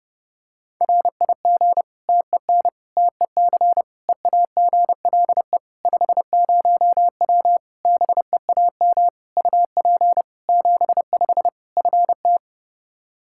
Morse code, heard as a keyed tone, about 32 wpm, 700 Hz. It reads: RIG TEN TEC EAGLE 50W BEAM UP 75 FT
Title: Practice QSO - Exchange 2 - Info - 30wpm